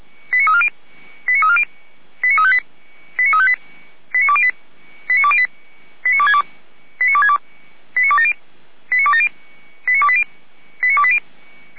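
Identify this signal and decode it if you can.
Five-tone selective call (ZVEI), heard as quick runs of five tones, tones of about 70 ms. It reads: 89340 89340 89358 89358 89208 89208 89262 89262 89270 89280 89280
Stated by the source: ZVEI five-tone selective call received on an SDR by the Feuerwehr Orlamünde fire brigade.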